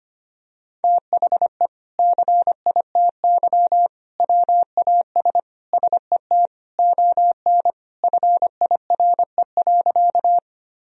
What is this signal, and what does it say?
Morse code, heard as a keyed tone, about 25 wpm, 700 Hz. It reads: THE CITY WAS SET ON FIRE.
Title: Sentences from Top 1000 Words - Review Part 1 - 25wpm